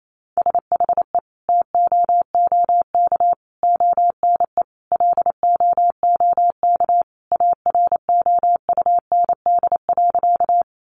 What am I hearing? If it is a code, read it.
Morse code, heard as a keyed tone, about 28 wpm, 700 Hz. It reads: SHE TOOK ONE LOOK AROUND.
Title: Sentences from Top 600 Words 28wpm